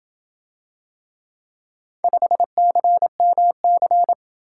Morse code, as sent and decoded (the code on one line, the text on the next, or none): ..... -.-. -- -.-.
5CMC